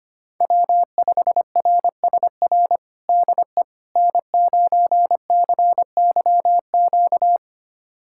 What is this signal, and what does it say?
Morse code, heard as a keyed tone, about 25 wpm, 700 Hz. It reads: W5RSR DE N9CYQ
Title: USA Callsign DE Callsign - 5 Characters - 25wpm